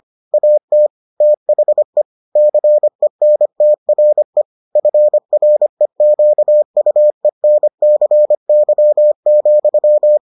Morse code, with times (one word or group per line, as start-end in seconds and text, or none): 0.3-0.9 AT
1.2-2.0 THE
2.4-4.4 CENTRE
4.8-10.2 FREQUENCY,